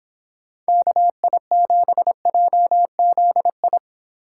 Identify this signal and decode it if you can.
Morse code, heard as a keyed tone, about 26 wpm, 700 Hz. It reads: KI7JZI
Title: Speed Racing - USA 2x3 Call Signs 22wpm